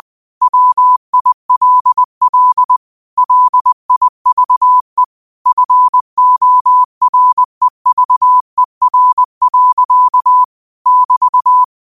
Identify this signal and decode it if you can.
Morse code, heard as a keyed tone, about 20 wpm, 1000 Hz. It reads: WILL LIVE FOREVER. <BT>
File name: qotdfiles / 20wpm